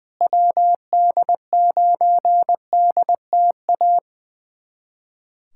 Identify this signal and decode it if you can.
Morse code, heard as a keyed tone, about 21 wpm, 700 Hz. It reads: WD9DTA
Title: USA Callsign DE Callsign - 6 Characters - 20wpm